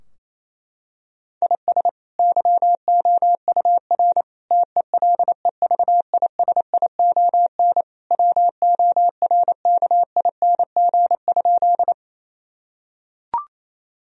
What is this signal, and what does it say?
Morse code, heard as a keyed tone, about 28 wpm, 700 Hz. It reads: IS YOUR TELEVISION WORKING?
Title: Sentences from Top 1000 Words - Review Part 4 - 28wpm